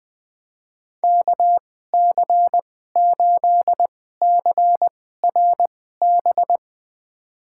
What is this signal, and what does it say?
Morse code, heard as a keyed tone, about 20 wpm, spaced as for 14 wpm, 700 Hz. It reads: KC8CRB
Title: USA Call Signs - 6 Characters - 2x Character Spacing - 20wpm